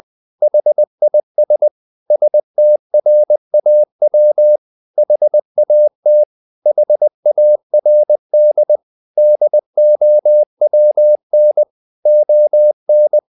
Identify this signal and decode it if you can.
Morse code, heard as a keyed tone, about 20 wpm, 600 Hz. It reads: HIS STRAW HAT HARD DOWN ON
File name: the old man and the sea / the old man and the sea part 1 20wpm